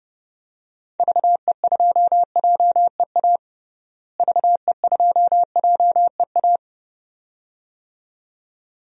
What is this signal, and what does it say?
Morse code, heard as a keyed tone, about 30 wpm, 700 Hz. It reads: VE2JEA VE2JEA
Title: DXCC-100 Call Signs - 6 Characters - Repeated 2x - 30wpm